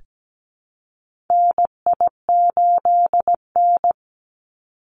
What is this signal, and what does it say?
Morse code, heard as a keyed tone, about 17 wpm, 700 Hz. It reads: NI8N
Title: Call Signs Repeated 2x - 17wpm